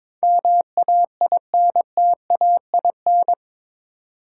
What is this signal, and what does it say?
Morse code, heard as a keyed tone, about 22 wpm, 700 Hz. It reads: MAINTAIN
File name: practice / 22wpm / Eight Letter Words - 22wpm